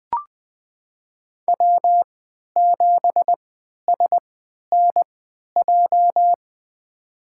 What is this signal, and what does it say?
Morse code, heard as a keyed tone, about 20 wpm, spaced as for 11 wpm, 700 Hz. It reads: W7SNJ
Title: USA Call Signs - 5 Characters - 3x Character Spacing - 20wpm